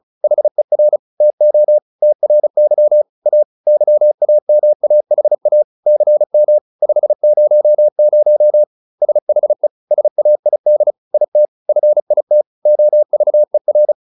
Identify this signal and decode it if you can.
Morse code, heard as a keyed tone, about 36 wpm, 600 Hz. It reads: HER TO TRY A YAMAHA CM 500 SHE SAID IT FIT OVER
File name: SS303 35